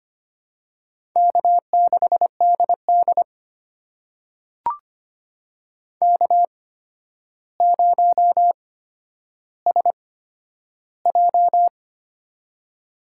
Morse code, heard as a keyed tone, about 25 wpm, spaced as for 11 wpm, 700 Hz. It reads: K6DD K0SJ
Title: USA Call Signs - 4 Characters - 8x Character Spacing - 25wpm